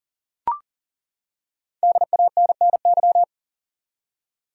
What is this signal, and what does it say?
Morse code, heard as a keyed tone, about 40 wpm, 700 Hz. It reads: DANNY
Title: USA Common Names - 40wpm